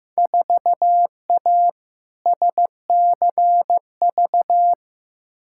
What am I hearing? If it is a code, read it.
Morse code, heard as a keyed tone, about 15 wpm, 700 Hz. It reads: 4A SCV
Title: Field Day Exchanges - 15wpm